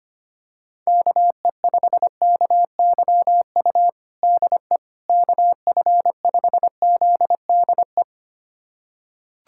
Morse code, heard as a keyed tone, about 25 wpm, 700 Hz. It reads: KE5KYU DE KF5ZDE
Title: USA Callsign DE Callsign 25wpm